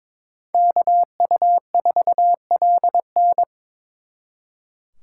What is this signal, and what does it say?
Morse code, heard as a keyed tone, about 26 wpm, 700 Hz. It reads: KU4LN
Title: Speed Racing - USA 2x2 Call Signs 22wpm